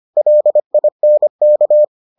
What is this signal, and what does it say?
Morse code, heard as a keyed tone, about 25 wpm, 600 Hz. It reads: LINK